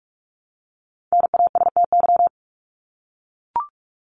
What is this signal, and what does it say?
Morse code, heard as a keyed tone, about 44 wpm, 700 Hz. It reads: NASTY
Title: Top 1751-2000 Words 45wpm